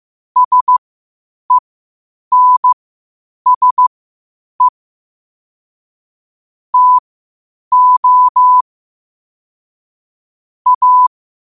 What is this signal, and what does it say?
Morse code, heard as a keyed tone, about 15 wpm, spaced as for 8 wpm, 1000 Hz. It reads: SENSE TO A
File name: qotdfiles / 8wpm